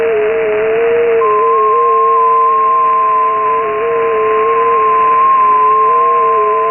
FT8.
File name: Radio / SDR Audio / gqrx